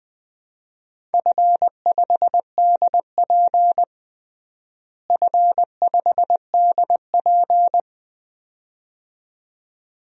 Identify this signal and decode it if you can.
Morse code, heard as a keyed tone, about 20 wpm, 700 Hz. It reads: F5DP F5DP
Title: DXCC-100 Call Signs - 4 Characters - Repeated 2x - 20wpm